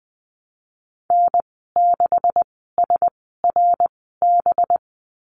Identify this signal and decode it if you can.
Morse code, heard as a keyed tone, about 20 wpm, spaced as for 15 wpm, 700 Hz. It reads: N6SRB
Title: 193 Lesson 31g - Letter v - Call Signs 20wpm